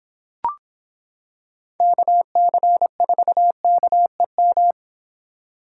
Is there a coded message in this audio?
Yes: Morse code from 1.8 to 4.7 s, heard as a keyed tone, about 26 wpm, 700 Hz. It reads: KC4KEM